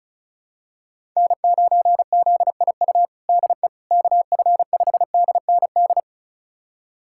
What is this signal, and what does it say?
Morse code, heard as a keyed tone, about 36 wpm, 700 Hz. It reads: N9ZIU DE KF5DND
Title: USA Callsign DE Callsign 35wpm